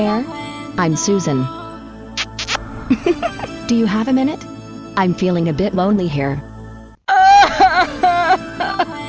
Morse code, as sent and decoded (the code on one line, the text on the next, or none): none